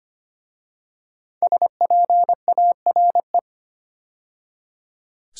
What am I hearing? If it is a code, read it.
Morse code, heard as a keyed tone, about 25 wpm, 700 Hz. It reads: SPARE
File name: 5 Letter Words - First 3 Letters Given - 25wpm